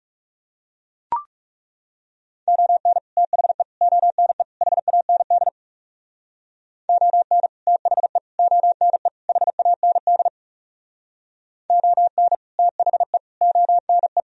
Morse code, heard as a keyed tone, about 36 wpm, 700 Hz. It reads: ON THE ONE HAND ON THE ONE HAND ON THE ONE
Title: Speed-Racing - Sets of 4 Words 30wpm